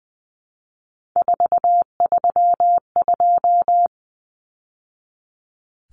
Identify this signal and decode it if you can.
Morse code, heard as a keyed tone, about 20 wpm, 700 Hz. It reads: 432
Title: Numbers - 3-Digit 20wpm